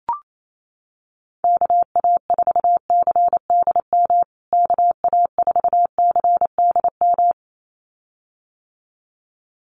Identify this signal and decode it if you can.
Morse code, heard as a keyed tone, about 28 wpm, 700 Hz. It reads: KA4CDM KA4CDM
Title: USA Call Signs Repeated 2x 28wpm